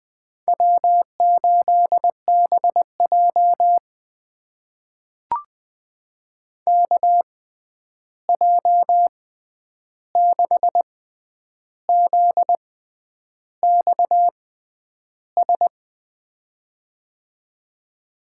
Morse code, heard as a keyed tone, about 20 wpm, spaced as for 9 wpm, 700 Hz. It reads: W8BJ KJ6ZXS
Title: USA Call Signs - 6x Character Spacing - 20wpm